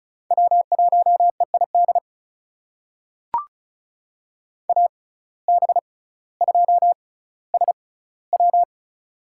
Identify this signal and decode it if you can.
Morse code, heard as a keyed tone, about 36 wpm, spaced as for 18 wpm, 700 Hz. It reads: W1EID AB2SW
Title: USA Call Signs - 6x Character Spacing - 35wpm